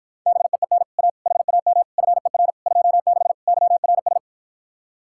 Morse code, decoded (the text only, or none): BEEN A HAM FER 26 YRS